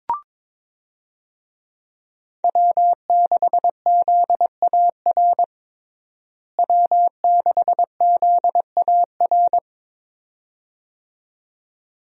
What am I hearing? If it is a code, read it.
Morse code, heard as a keyed tone, about 22 wpm, 700 Hz. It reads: W6ZAR W6ZAR
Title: USA Call Signs - 5 Characters - Repeated 2x - 22wpm